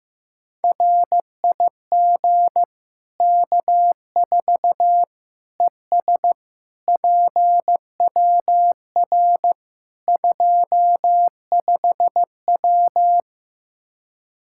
Morse code, heard as a keyed tone, about 15 wpm, 700 Hz. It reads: RIG K4 ES PWR 25W
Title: Practice QSO - Exchange 2 - Info - 15wpm